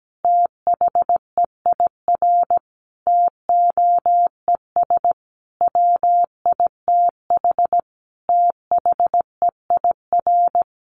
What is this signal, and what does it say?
Morse code, heard as a keyed tone, about 17 wpm, 700 Hz. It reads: THEIR TOES WITH THEIR
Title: Easy Phrases 17wpm